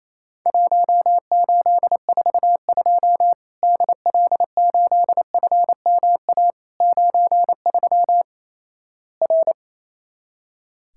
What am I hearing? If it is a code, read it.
Morse code, heard as a keyed tone, about 28 wpm, 700 Hz. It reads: 1842 DL8FMA 93 R
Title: Worked All Europe DX Contest - QTC Exchange - 28wpm